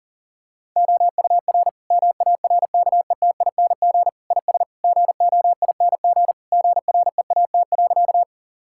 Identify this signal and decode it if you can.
Morse code, heard as a keyed tone, about 40 wpm, 700 Hz. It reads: OUR MARKETING IS GOING GREAT.
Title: Sentences from Top 1000 Words - Review Part 5 - 40wpm